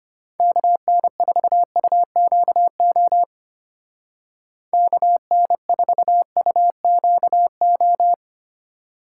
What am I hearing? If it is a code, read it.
Morse code, heard as a keyed tone, about 30 wpm, 700 Hz. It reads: KN4UQO KN4UQO
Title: Speed Racing - USA 2x3 Call Signs 25wpm